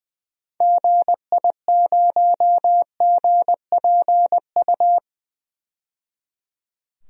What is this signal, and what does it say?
Morse code, heard as a keyed tone, about 20 wpm, 700 Hz. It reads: GI0GPU
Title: DXCC-100 Call Signs - 6 Characters - 20wpm